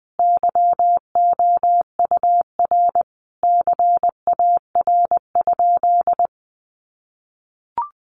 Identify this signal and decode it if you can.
Morse code, heard as a keyed tone, about 20 wpm, 700 Hz. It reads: YOUR CAR?
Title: Sentences from Top 1500 Words 20wpm